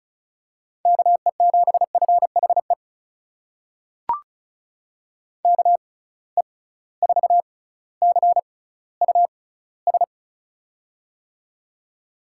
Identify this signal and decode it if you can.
Morse code, heard as a keyed tone, about 34 wpm, spaced as for 18 wpm, 700 Hz. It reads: KE7FHE KE4CUS
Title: USA Call Signs - 6 Characters - 6x Character Spacing - 35wpm